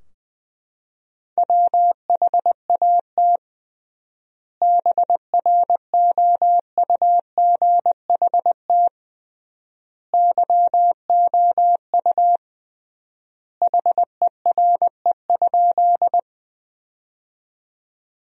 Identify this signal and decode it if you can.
Morse code, heard as a keyed tone, about 20 wpm, 700 Hz. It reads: WHAT BROUGHT YOU HERE?
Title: 3x Word-Spacing - Sentences from Top 500 Words - 20wpm